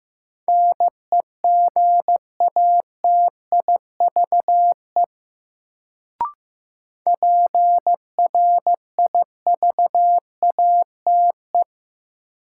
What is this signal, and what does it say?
Morse code, heard as a keyed tone, about 16 wpm, 700 Hz. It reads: NEGATIVE PRIVATE
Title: Sentences from Top 1000 Words - New Word Drill - 15wpm